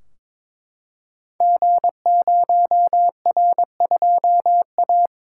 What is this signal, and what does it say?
Morse code, heard as a keyed tone, about 22 wpm, 700 Hz. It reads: G0R2A
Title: Code Groups - 5 Letter:Numbers - 22wpm